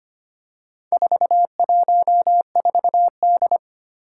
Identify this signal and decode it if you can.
Morse code, heard as a keyed tone, about 25 wpm, 700 Hz. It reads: 414D
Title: Four Challenging Letter-Number Combinations 25wpm